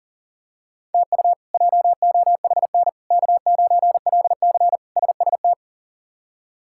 Morse code, heard as a keyed tone, about 40 wpm, 700 Hz. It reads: TU JOHN K9LC SST
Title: K1USN SST QSO Practice - 40wpm